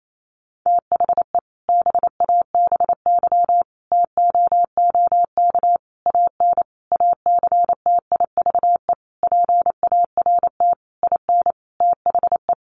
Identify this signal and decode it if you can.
Morse code, heard as a keyed tone, about 28 wpm, 700 Hz. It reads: THE BABY TOOK AN ACTIVE PART IN THE
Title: Sentences from Top 1300 Words 28wpm